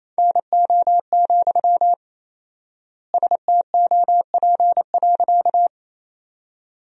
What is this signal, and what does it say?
Morse code, heard as a keyed tone, about 28 wpm, 700 Hz. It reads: NO, STOP.
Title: 4x Word-Spacing - Sentences from Top 300 Words - 28wpm